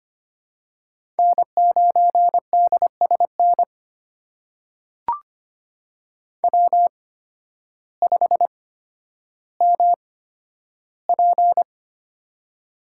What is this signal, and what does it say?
Morse code, heard as a keyed tone, about 25 wpm, spaced as for 12 wpm, 700 Hz. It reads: N9DSN W5MP
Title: USA Call Signs - 8x Character Spacing - 25wpm